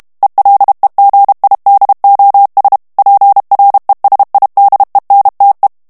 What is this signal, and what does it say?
Morse code, heard as a keyed tone, about 32 wpm, 800 Hz. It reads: ELEGIDOS PRESIDENTE
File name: morse mp3 / ESPAGNOL TEXTE EN CLAIR 1 VITESSE 30 MOTS MINUTES